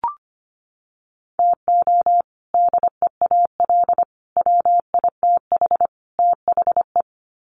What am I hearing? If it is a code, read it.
Morse code, heard as a keyed tone, about 25 wpm, 700 Hz. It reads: TO DEAL WITH THE
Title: Sets of 4 Words 25wpm